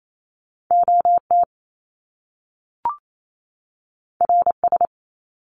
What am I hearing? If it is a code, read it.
Morse code, heard as a keyed tone, about 28 wpm, 700 Hz. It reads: OT RS
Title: Top Two Letter Combinations in Words 28wpm